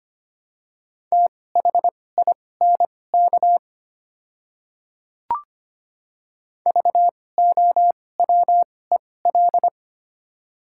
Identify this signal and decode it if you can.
Morse code, heard as a keyed tone, about 25 wpm, spaced as for 18 wpm, 700 Hz. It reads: THINK VOWEL
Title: 2x Character Spacing - Top 500 Words - 25wpm